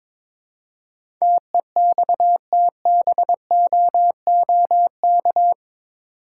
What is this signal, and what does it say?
Morse code, heard as a keyed tone, about 22 wpm, 700 Hz. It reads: TEXTBOOK
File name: practice / 22wpm / Eight Letter Words - 22wpm